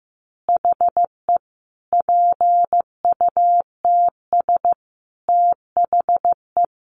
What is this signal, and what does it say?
Morse code, heard as a keyed tone, about 15 wpm, 700 Hz. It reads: HE PUTS THE
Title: Additional Easy Phrases 15wpm